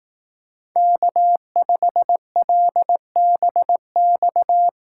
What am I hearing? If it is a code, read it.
Morse code, heard as a keyed tone, about 18 wpm, 700 Hz. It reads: K5LBX
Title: Speed Racing - USA 1x3 Call Signs 15wpm